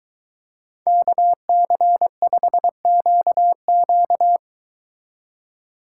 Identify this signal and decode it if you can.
Morse code, heard as a keyed tone, about 22 wpm, 700 Hz. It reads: KC5QQ